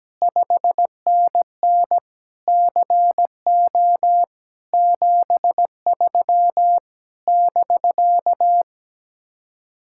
Morse code, heard as a keyed tone, about 17 wpm, 700 Hz. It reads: 5NN CO 73 <BK>
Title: Speed-Racing Common QSO Phrases Encore - 17wpm